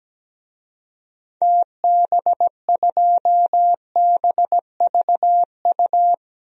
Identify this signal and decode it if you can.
Morse code, heard as a keyed tone, about 17 wpm, 700 Hz. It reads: TB2BVU
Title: DXCC-100 Call Signs - 6 Characters - 17wpm